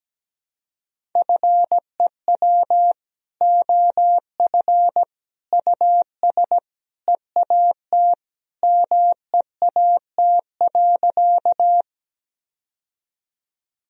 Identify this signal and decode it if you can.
Morse code, heard as a keyed tone, about 17 wpm, 700 Hz. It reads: FEW OF US EAT MEAT.
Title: Sentences from Top 1000 Words - Review Part 3 - 17wpm